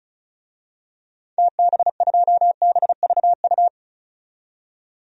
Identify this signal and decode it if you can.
Morse code, heard as a keyed tone, about 34 wpm, 700 Hz. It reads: TB2BVU
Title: DXCC-100 Call Signs - 6 Characters - 35wpm